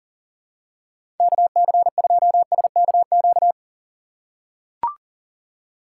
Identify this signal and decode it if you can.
Morse code, heard as a keyed tone, about 40 wpm, 700 Hz. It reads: KC2SKQ
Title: USA Call Signs - 6 Characters - 2x Character Spacing - 40wpm